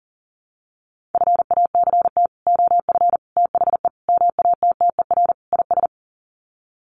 Morse code, heard as a keyed tone, about 40 wpm, 700 Hz. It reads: FACT OF THE MATTER IS
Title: Sets of 5 Words 40wpm